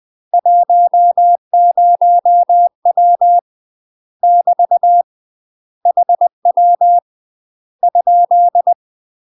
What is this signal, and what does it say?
Morse code, heard as a keyed tone, about 20 wpm, 700 Hz. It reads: 10W <BT> HW ?
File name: UK MASTER QSOs TRAINING/20WPM